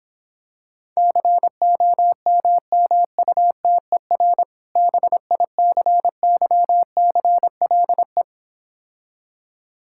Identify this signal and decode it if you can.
Morse code, heard as a keyed tone, about 26 wpm, 700 Hz. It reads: COMMUTER BICYCLE